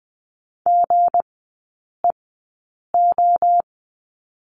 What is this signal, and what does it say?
Morse code, heard as a keyed tone, about 20 wpm, spaced as for 8 wpm, 700 Hz. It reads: GEO
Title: 184 Lesson 30e - Letter b - Three Characters 20wpm